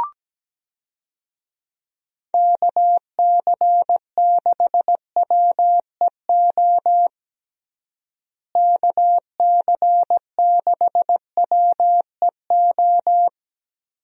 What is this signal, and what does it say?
Morse code, heard as a keyed tone, about 17 wpm, 700 Hz. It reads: KC6WEO KC6WEO
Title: USA Call Signs - 6 Characters - Repeated 2x - 17wpm